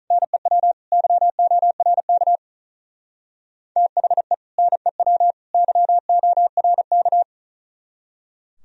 Morse code, heard as a keyed tone, about 38 wpm, 700 Hz. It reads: NEW YORK THE NEW YORK
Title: Speed-Racing - Sets of 3 Words 35wpm